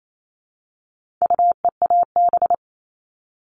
Morse code, heard as a keyed tone, about 28 wpm, 700 Hz. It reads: UEAB
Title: All Cut Numbers - 4-Digit 28wpm